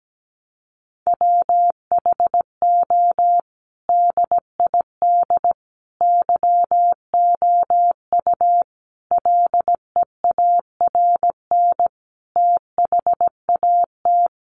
Morse code, heard as a keyed tone, about 17 wpm, 700 Hz. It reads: WHO DID YOU LEARN THAT
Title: Sentences from Top 300 Words 17wpm